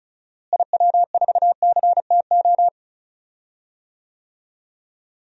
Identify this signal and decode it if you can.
Morse code, heard as a keyed tone, about 34 wpm, 700 Hz. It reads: IW4CTO
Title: DXCC-100 Call Signs - 6 Characters - Repeated 3x - 35wpm